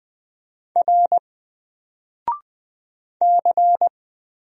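Morse code, heard as a keyed tone, about 20 wpm, 700 Hz. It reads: R C
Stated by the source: Single letter and number Morse drill at 20 wpm, from N4LCD.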